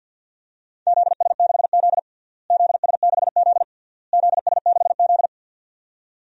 Morse code, heard as a keyed tone, about 52 wpm, 700 Hz. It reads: ZS6Z ZS6Z ZS6Z
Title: DXCC-100 Call Signs - 4 Characters - Repeated 3x - 50wpm